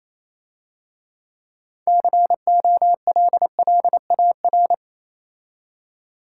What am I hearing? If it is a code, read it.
Morse code, heard as a keyed tone, about 27 wpm, 700 Hz. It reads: COLLAR
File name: Easy Words - Speed-Racing with Hints - 22wpm